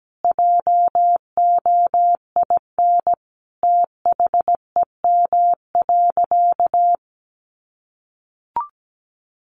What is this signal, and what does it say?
Morse code, heard as a keyed tone, about 17 wpm, 700 Hz. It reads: JOIN THEM.
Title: Sentences from Top 1400 Words 17wpm